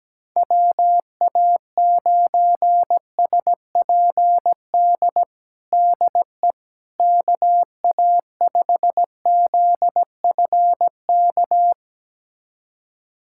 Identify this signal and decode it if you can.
Morse code, heard as a keyed tone, about 17 wpm, 700 Hz. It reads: WA9SPD DE KA5ZFK
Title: USA Callsign DE Callsign - 6 Characters - 17wpm